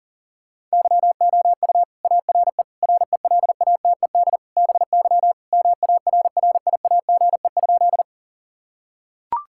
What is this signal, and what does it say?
Morse code, heard as a keyed tone, about 40 wpm, 700 Hz. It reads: YOU ARE RELATED BY MARRIAGE?
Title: Sentences from Top 1000 Words - Review Part 5 - 40wpm